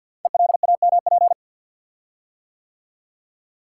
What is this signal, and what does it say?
Morse code, heard as a keyed tone, about 48 wpm, 700 Hz. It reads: ELAMP
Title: Code Groups - 5 Letters - 50wpm